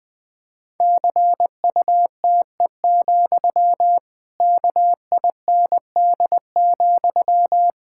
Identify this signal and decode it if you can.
Morse code, heard as a keyed tone, about 21 wpm, 700 Hz. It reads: CUTE, KIND,
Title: Pangram Challenge v2 - 20wpm